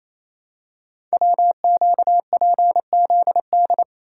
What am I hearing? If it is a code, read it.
Morse code, heard as a keyed tone, about 27 wpm, 700 Hz. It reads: WQPZD
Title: Code Groups - 5 Letter:Numbers - 28wpm